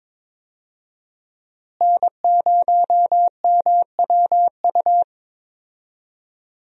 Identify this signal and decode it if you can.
Morse code, heard as a keyed tone, about 22 wpm, 700 Hz. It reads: N0MWU
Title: USA Call Signs - 5 Characters - Repeated 3x - 22wpm